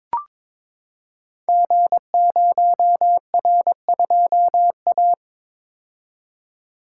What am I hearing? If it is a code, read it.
Morse code, heard as a keyed tone, about 22 wpm, 700 Hz. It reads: G0R2A
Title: Code Groups - 5 Letter:Numbers - 22wpm